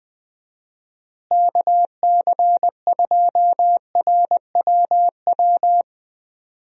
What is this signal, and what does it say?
Morse code, heard as a keyed tone, about 20 wpm, 700 Hz. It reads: KC2RWW
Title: USA Call Signs - 6 Characters - Repeated 2x - 20wpm